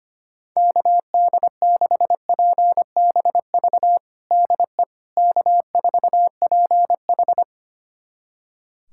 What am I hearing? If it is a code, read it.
Morse code, heard as a keyed tone, about 25 wpm, 700 Hz. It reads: KD6PBV DE K4PH
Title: USA Callsign DE Callsign 25wpm